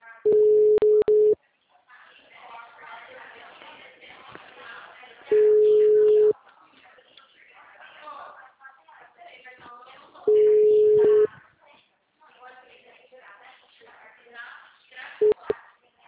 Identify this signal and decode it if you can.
Ringback tone (Europe, 425 Hz), with speech.